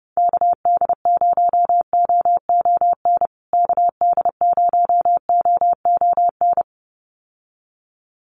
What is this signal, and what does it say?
Morse code, heard as a keyed tone, about 30 wpm, 700 Hz. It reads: KD0OON KD0OON
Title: USA Call Signs Repeated 2x 30wpm